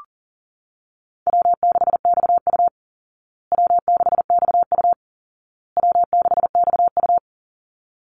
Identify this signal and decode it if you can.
Morse code, heard as a keyed tone, about 40 wpm, 700 Hz. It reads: W6XU W6XU W6XU
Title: Easy Call Signs Repeated 3x - 40wpm